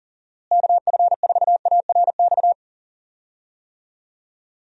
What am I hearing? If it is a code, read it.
Morse code, heard as a keyed tone, about 40 wpm, 700 Hz. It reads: KF4ARX
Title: USA Call Signs - 6 Characters - Repeated 2x - 40wpm